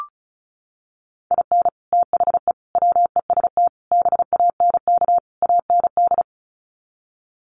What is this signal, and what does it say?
Morse code, heard as a keyed tone, about 34 wpm, 700 Hz. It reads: IN THE WEST BANK AND
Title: Sets of 5 Words Encore2 35wpm